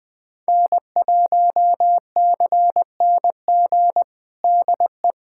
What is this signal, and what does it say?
Morse code, heard as a keyed tone, about 20 wpm, 700 Hz. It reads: N1CNG DE
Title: USA Callsign DE Callsign - 5 Characters - 20wpm